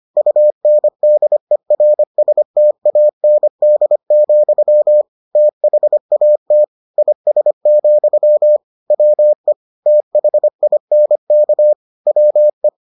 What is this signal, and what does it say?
Morse code, heard as a keyed tone, about 25 wpm, 600 Hz. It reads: UNDERSTAND, THAT IS, WE THINK WE